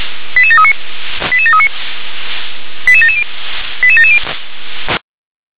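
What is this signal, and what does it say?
Five-tone selective call (ZVEI), heard as quick runs of five tones, tones of about 70 ms. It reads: 88739 88739 88770 88770
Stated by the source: ZVEI five-tone selective call received on SDR.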